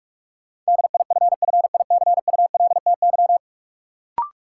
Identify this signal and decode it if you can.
Morse code, heard as a keyed tone, about 46 wpm, 700 Hz. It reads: DIFFICULTY
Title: Sentences from Top 1500 Words - New Word Drill Alt - 45wpm